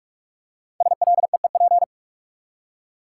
Morse code, heard as a keyed tone, about 48 wpm, 700 Hz. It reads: SLEEP
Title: Five Letter Words - 45wpm